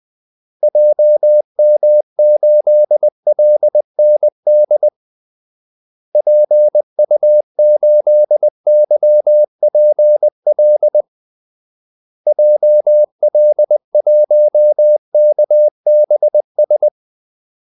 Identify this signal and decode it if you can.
Morse code, heard as a keyed tone, about 20 wpm, 600 Hz. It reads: JM8LND PU8YPL JL1KBS
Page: Kaldesignaler 20 wpm